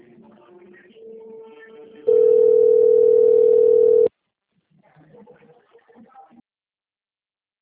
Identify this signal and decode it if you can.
Ringback tone (North America), 440 + 480 Hz.